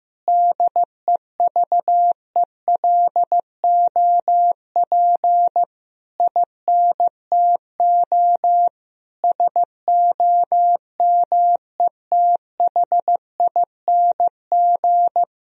Morse code, heard as a keyed tone, about 15 wpm, 700 Hz. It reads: DEVELOP INTO SOMETHING
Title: Sentences from Top 1000 Words - Review Part 2 - 15wpm